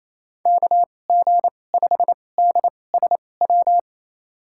Morse code, heard as a keyed tone, about 28 wpm, spaced as for 20 wpm, 700 Hz. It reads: KG5DSW
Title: USA Call Signs - 6 Characters - 2x Character Spacing - 28wpm